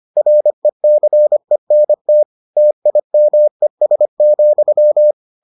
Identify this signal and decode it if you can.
Morse code, heard as a keyed tone, about 25 wpm, 600 Hz. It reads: RECENT TIMES,